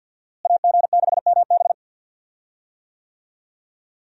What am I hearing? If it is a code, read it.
Morse code, heard as a keyed tone, about 52 wpm, 700 Hz. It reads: AG6MB
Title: USA Call Signs - 5 Characters - Repeated 2x - 50wpm